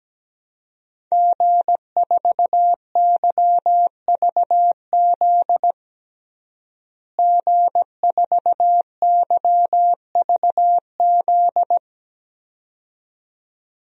Morse code, heard as a keyed tone, about 17 wpm, 700 Hz. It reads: G4YVZ G4YVZ
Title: DXCC-100 Call Signs - 5 Characters - Repeated 2x - 17wpm